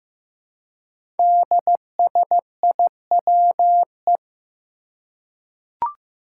Morse code, heard as a keyed tone, about 15 wpm, 700 Hz. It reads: DSIWE
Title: Five Challenging Letter-Number Combinations 15wpm